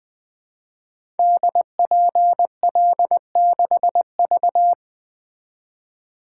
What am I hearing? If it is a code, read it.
Morse code, heard as a keyed tone, about 20 wpm, 700 Hz. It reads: DPL6V
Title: Code Groups - 5 Letter:Numbers - 20wpm